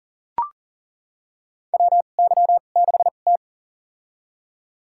Morse code, heard as a keyed tone, about 40 wpm, spaced as for 29 wpm, 700 Hz. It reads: WY6T
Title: USA Call Signs - 4 Characters - 2x Character Spacing - 40wpm